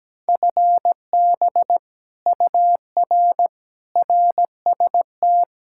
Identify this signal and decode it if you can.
Morse code, heard as a keyed tone, about 17 wpm, 700 Hz. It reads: FB UR RST
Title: Speed-Racing Common QSO Phrases Encore - 17wpm